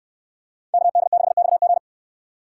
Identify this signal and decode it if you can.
Morse code, heard as a keyed tone, about 65 wpm, 700 Hz. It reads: KD6XG